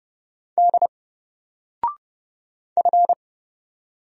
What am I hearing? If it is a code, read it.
Morse code, heard as a keyed tone, about 30 wpm, 700 Hz. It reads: D F
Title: Single Character (Letter:Number:Punctuation:Prosign) 30wpm